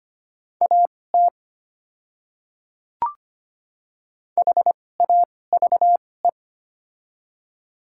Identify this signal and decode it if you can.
Morse code, heard as a keyed tone, about 25 wpm, spaced as for 18 wpm, 700 Hz. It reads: AT HAVE
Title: 2x Character Spacing - Top 100 Words - 25wpm